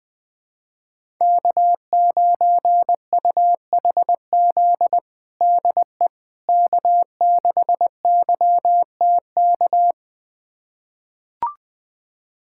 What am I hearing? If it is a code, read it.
Morse code, heard as a keyed tone, about 21 wpm, 700 Hz. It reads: K9UHZ DE K6YTK
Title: USA Callsign DE Callsign - 5 Characters - 20wpm